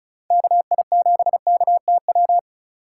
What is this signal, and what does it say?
Morse code, heard as a keyed tone, about 44 wpm, 700 Hz. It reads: KI7KTW